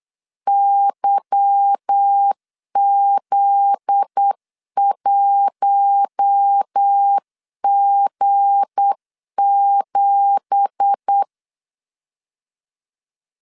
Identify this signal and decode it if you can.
Morse code, heard as a keyed tone, about 8 wpm, 800 Hz. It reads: YZ1G7
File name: CW / DET / 1 9 1500C S A 8wpm 5637